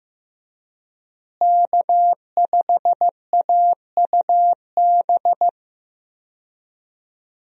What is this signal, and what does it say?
Morse code, heard as a keyed tone, about 15 wpm, 700 Hz. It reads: K5AUB
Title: USA Call Signs - 5 Characters - Repeated 2x - 15wpm